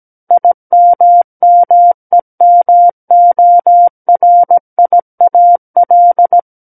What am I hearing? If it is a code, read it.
Morse code, heard as a keyed tone, about 17 wpm, 700 Hz. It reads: IMMEMORIAL
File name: Book One / 08-Chapter 8 - 17 WPM